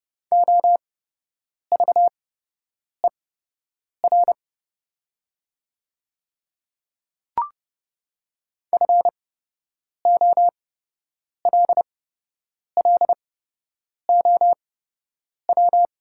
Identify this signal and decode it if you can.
Morse code, heard as a keyed tone, about 30 wpm, spaced as for 8 wpm, 700 Hz. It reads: OVER FOLLOW
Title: 8x Character Spacing - Top 500 Words - 30wpm